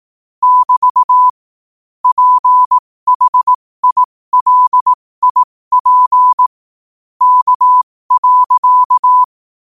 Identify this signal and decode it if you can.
Morse code, heard as a keyed tone, about 18 wpm, spaced as for 15 wpm, 1000 Hz. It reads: <BT> PHILIP K.
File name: qotdfiles / 15wpm